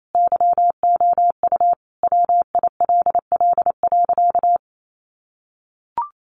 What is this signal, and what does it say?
Morse code, heard as a keyed tone, about 28 wpm, 700 Hz. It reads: YOU WILL.
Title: Sentences from Top 1400 Words 28wpm